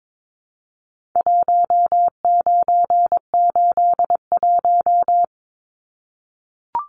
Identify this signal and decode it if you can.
Morse code, heard as a keyed tone, about 22 wpm, 700 Hz. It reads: 1981
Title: Numbers - Contest Style - 4-Digit 22wpm